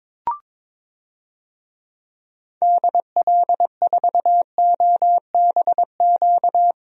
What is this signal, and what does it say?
Morse code, heard as a keyed tone, about 22 wpm, 700 Hz. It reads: DL4OBQ
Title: USA Call Signs - 6 Characters - Repeated 3x - 22wpm